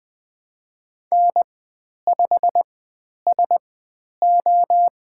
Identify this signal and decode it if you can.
Morse code, heard as a keyed tone, about 20 wpm, spaced as for 10 wpm, 700 Hz. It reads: N5SO
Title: 102 Lesson 17g - Letter u - Call Signs 20wpm